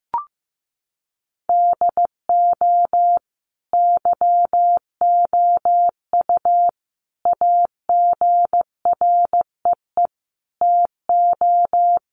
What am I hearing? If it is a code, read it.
Morse code, heard as a keyed tone, about 15 wpm, 700 Hz. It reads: DO YOU AGREE TO
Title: Sentences from Top 1100 Words 15wpm